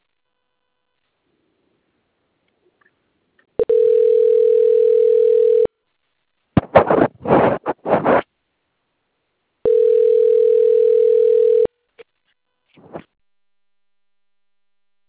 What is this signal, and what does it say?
Ringback tone (North America), 440 + 480 Hz.